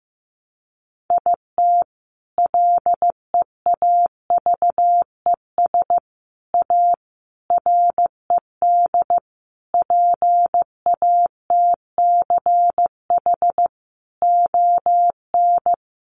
Morse code, heard as a keyed tone, about 15 wpm, 700 Hz. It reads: IT LEAVES A RED PATCH ON
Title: A Sound Of Thunder 15wpm